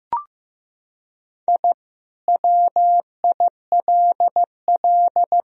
Morse code, heard as a keyed tone, about 15 wpm, 700 Hz. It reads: I WILL
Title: Sentences from Top 1000 Words - Review Part 2 - 15wpm